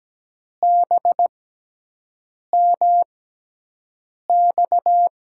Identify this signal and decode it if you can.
Morse code, heard as a keyed tone, about 17 wpm, spaced as for 6 wpm, 700 Hz. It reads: BMX
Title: USA Call Signs - 5 Characters - 6x Character Spacing - 17wpm